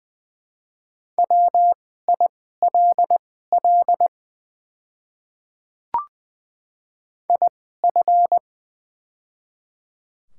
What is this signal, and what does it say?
Morse code, heard as a keyed tone, about 20 wpm, spaced as for 14 wpm, 700 Hz. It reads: WILL IF
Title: 2x Character Spacing - Top 200 Words - 20wpm